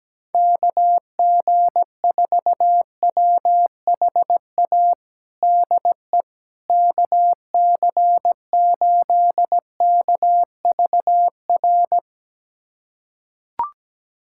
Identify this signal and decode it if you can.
Morse code, heard as a keyed tone, about 17 wpm, 700 Hz. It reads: KG4WHA DE KC8KVR
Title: USA Callsign DE Callsign - 6 Characters - 17wpm